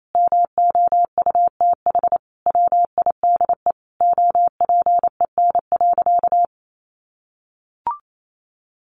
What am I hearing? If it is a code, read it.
Morse code, heard as a keyed tone, about 28 wpm, 700 Hz. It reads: MOUTH WIDE OPEN.
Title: Sentences from Top 1500 Words 28wpm